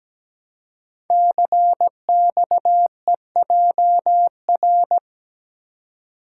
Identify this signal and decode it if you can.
Morse code, heard as a keyed tone, about 17 wpm, 700 Hz. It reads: CXEJR
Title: Code Groups - 5 Letter:Numbers - 17wpm